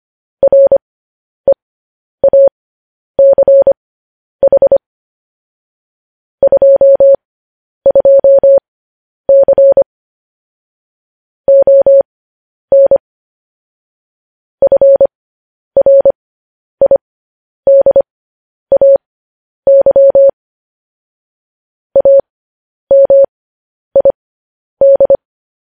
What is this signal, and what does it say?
Morse code, heard as a keyed tone, about 25 wpm, spaced as for 10 wpm, 550 Hz. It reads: REACH 22C ON FRIDAY AMID
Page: News Headlines in Morse Code at 10 WPM